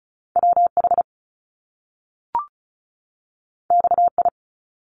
Morse code, heard as a keyed tone, about 36 wpm, 700 Hz. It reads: WH XI
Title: Top Two Letter Combinations in Words 35wpm